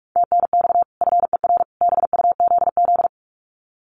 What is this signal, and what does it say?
Morse code, heard as a keyed tone, about 44 wpm, 700 Hz. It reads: TNX FER BUZZ
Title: Common QSO Phrases 45wpm